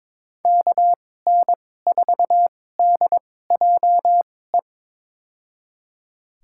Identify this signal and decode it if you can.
Morse code, heard as a keyed tone, about 22 wpm, spaced as for 16 wpm, 700 Hz. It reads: KN4DJE
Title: USA Call Signs - 6 Characters - 2x Character Spacing - 22wpm